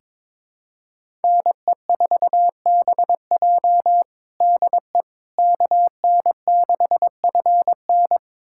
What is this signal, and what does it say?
Morse code, heard as a keyed tone, about 22 wpm, 700 Hz. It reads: NE4BJ DE KN6FN
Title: USA Callsign DE Callsign - 5 Characters - 22wpm